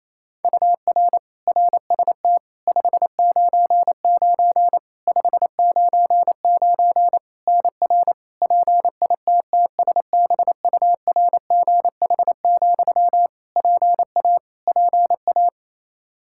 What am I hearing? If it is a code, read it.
Morse code, heard as a keyed tone, about 28 wpm, 700 Hz. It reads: UR RST 599 599 NR PITTSBURGH, PA PA
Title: Practice QSO - Exchange 1 - Beginning - 28wpm